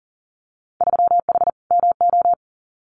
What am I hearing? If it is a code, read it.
Morse code, heard as a keyed tone, about 40 wpm, 700 Hz. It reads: 3H MO
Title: Winter Field Day Exchanges Encore 40wpm